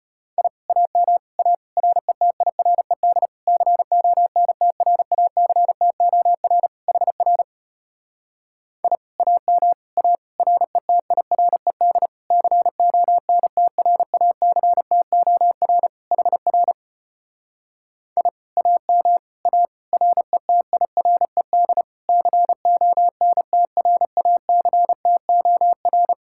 Morse code, heard as a keyed tone, about 33 wpm, 700 Hz. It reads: I AM A RETIRED CONTRACTOR HR I AM A RETIRED CONTRACTOR HR I AM A RETIRED CONTRACTOR